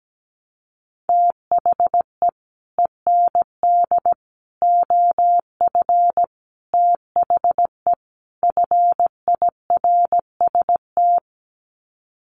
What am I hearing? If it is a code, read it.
Morse code, heard as a keyed tone, about 17 wpm, 700 Hz. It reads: THE END OF THE FIRST
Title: Sets of 5 Words Encore 17wpm